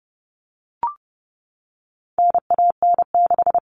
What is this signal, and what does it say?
Morse code, heard as a keyed tone, about 30 wpm, 700 Hz. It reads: NAN6
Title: All Cut Numbers - 4-Digit 30wpm